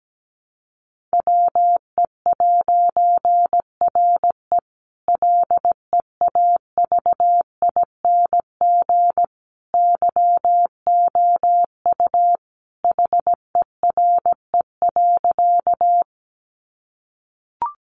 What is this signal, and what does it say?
Morse code, heard as a keyed tone, about 17 wpm, 700 Hz. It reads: WE'RE LEAVING YOU HERE.
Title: A Sound Of Thunder 17wpm